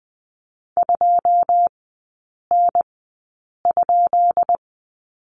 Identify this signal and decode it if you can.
Morse code, heard as a keyed tone, about 20 wpm, spaced as for 8 wpm, 700 Hz. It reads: 2N?
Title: 212 Lesson 34e - Number 8 - Three Characters 20wpm